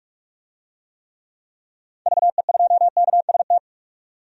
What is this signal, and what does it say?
Morse code, heard as a keyed tone, about 45 wpm, 700 Hz. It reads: VE2KST
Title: DXCC-100 Call Signs - 6 Characters - 45wpm